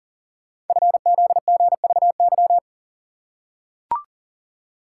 Morse code, heard as a keyed tone, about 40 wpm, 700 Hz. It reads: FZGVY
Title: Code Groups - 5 Letters - 40wpm